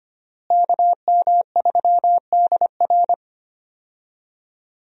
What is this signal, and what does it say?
Morse code, heard as a keyed tone, about 25 wpm, 700 Hz. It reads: KM3DR
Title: USA Call Signs - 5 Characters - Repeated 3x - 25wpm